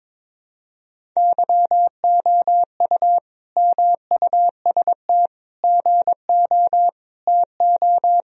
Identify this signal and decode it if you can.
Morse code, heard as a keyed tone, about 22 wpm, 700 Hz. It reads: YOU MUST GO TO
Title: Sentences from Top 1000 Words - Review Part 1 - 22wpm